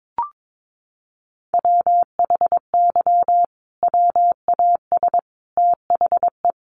Morse code, heard as a keyed tone, about 22 wpm, 700 Hz. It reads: WHY WAS THE
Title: Sentences from Top 1400 Words 22wpm